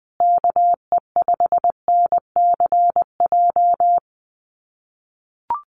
Morse code, heard as a keyed tone, about 20 wpm, 700 Hz. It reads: KE5NCJ
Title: USA Call Signs Repeated 3x 20wpm